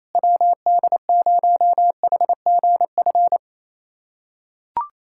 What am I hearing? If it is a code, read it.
Morse code, heard as a keyed tone, about 28 wpm, 700 Hz. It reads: WD0HGF
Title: USA Callsign DE Callsign - 6 Characters - 28wpm